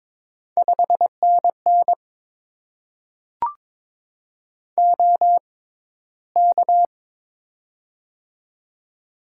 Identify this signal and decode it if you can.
Morse code, heard as a keyed tone, about 22 wpm, spaced as for 14 wpm, 700 Hz. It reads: 5NN OK
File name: Top 90 QSO Elements - 6x Character Spacing - 22wpm